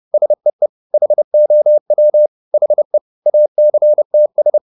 Morse code, heard as a keyed tone, about 30 wpm, 600 Hz. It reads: SEE HOW HE ACTS